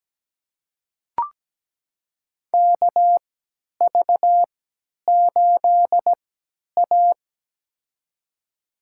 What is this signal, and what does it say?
Morse code, heard as a keyed tone, about 17 wpm, spaced as for 10 wpm, 700 Hz. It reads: KV8A
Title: USA Call Signs - 4 Characters - 3x Character Spacing - 17wpm